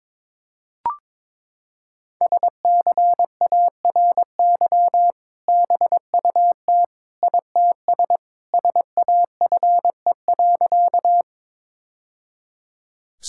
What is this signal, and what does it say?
Morse code, heard as a keyed tone, about 22 wpm, 700 Hz. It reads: SCARY BUT ITS SAFE.
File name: practice / 22wpm / Sentences with 2-Word Phrases - 22wpm